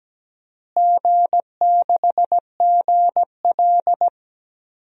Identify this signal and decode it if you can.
Morse code, heard as a keyed tone, about 17 wpm, 700 Hz. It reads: G6GL
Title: DXCC-100 Call Signs - 4 Characters - Repeated 2x - 17wpm